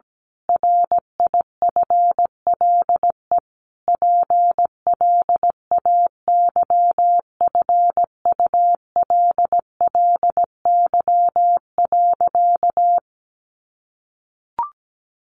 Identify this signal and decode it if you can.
Morse code, heard as a keyed tone, about 17 wpm, 700 Hz. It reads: RIFLE PLAYFULLY.
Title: A Sound Of Thunder 17wpm